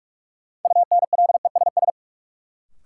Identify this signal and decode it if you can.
Morse code, heard as a keyed tone, about 42 wpm, 700 Hz. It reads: UNLESS
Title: Sentences from Top 1000 Words - Review Part 5 - New Word Drill - 45wpm